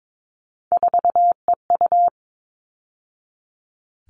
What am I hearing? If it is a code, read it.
Morse code, heard as a keyed tone, about 22 wpm, 700 Hz. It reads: 4EU
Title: All Cut Numbers - 3-Digit 22wpm